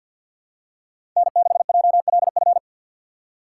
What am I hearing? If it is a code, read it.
Morse code, heard as a keyed tone, about 48 wpm, 700 Hz. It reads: N6JLF
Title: USA Call Signs - 5 Characters - 4x Character Spacing - 50wpm